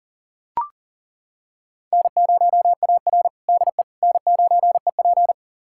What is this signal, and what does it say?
Morse code, heard as a keyed tone, about 40 wpm, 700 Hz. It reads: N0AR DE N9EP
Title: USA Callsign DE Callsign - 4 Characters - 40wpm